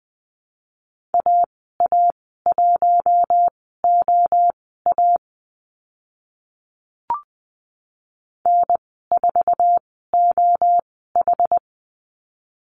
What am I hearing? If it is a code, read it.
Morse code, heard as a keyed tone, about 20 wpm, spaced as for 15 wpm, 700 Hz. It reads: AA1OA N4OH
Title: 082 Lesson 14g - Number 2 - Call Signs 20wpm